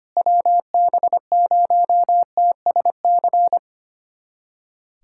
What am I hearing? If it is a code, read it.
Morse code, heard as a keyed tone, about 25 wpm, 700 Hz. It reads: WB0TSC
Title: USA Callsign DE Callsign 25wpm